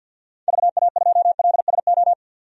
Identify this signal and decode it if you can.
Morse code, heard as a keyed tone, about 51 wpm, 700 Hz. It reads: VA2LSO
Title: DXCC-100 Call Signs - 6 Characters - 50wpm